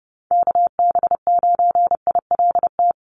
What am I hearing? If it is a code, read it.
Morse code, heard as a keyed tone, about 30 wpm, 700 Hz. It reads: KB9ILT
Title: NA SOTA Activator Callsigns 30wpm